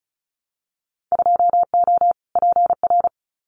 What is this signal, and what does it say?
Morse code, heard as a keyed tone, about 33 wpm, 700 Hz. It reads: 2O PR
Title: Winter Field Day Exchanges Encore 35wpm